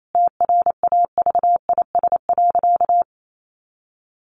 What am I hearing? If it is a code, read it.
Morse code, heard as a keyed tone, about 28 wpm, 700 Hz. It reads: TRAVIS.
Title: A Sound Of Thunder 28wpm